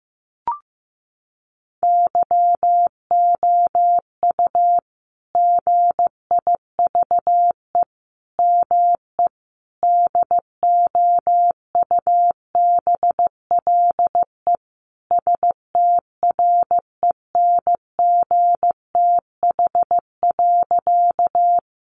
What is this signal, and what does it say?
Morse code, heard as a keyed tone, about 15 wpm, 700 Hz. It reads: YOU GIVE ME DOUBLE STRENGTH.
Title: Sentences from Top 1400 Words 15wpm